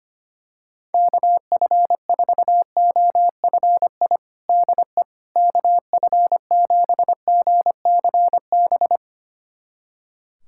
Morse code, heard as a keyed tone, about 25 wpm, 700 Hz. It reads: KF4OFI DE KF7GCB
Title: USA Callsign DE Callsign - 6 Characters - 25wpm